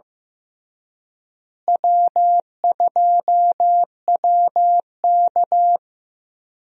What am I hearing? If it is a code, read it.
Morse code, heard as a keyed tone, about 15 wpm, 700 Hz. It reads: W2WK
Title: USA Call Signs - 4 Characters - Repeated 3x - 15wpm